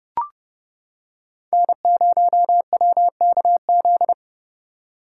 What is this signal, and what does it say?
Morse code, heard as a keyed tone, about 29 wpm, 700 Hz. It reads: N0WKZ